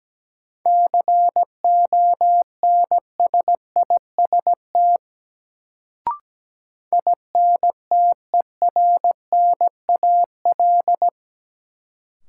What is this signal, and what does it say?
Morse code, heard as a keyed tone, about 17 wpm, 700 Hz. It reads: CONSIST INTERNAL
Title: Sentences from Top 1400 Words - New Word Drill - 17wpm